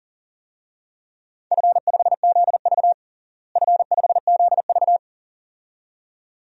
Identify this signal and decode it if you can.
Morse code, heard as a keyed tone, about 40 wpm, 700 Hz. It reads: F5ZV F5ZV
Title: DXCC-100 Call Signs - 4 Characters - Repeated 2x - 40wpm